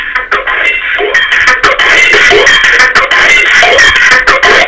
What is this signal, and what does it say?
Emergency Alert System (SAME) tones, heard as harsh buzzing bursts. It is heard from 0.5 to 3.6 s.